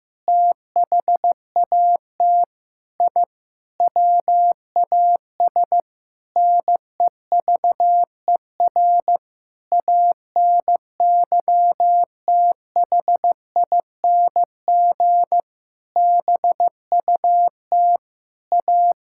Morse code, heard as a keyed tone, about 15 wpm, 700 Hz. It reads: THAT I WAS NEVER ANYTHING BUT A
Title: Sentences from Top 1000 Words - Review Part 6 - 15wpm